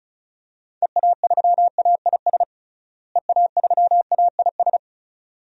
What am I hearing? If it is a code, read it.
Morse code, heard as a keyed tone, about 36 wpm, 700 Hz. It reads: EA3AIS EA3AIS
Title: DXCC-100 Call Signs - 6 Characters - Repeated 2x - 35wpm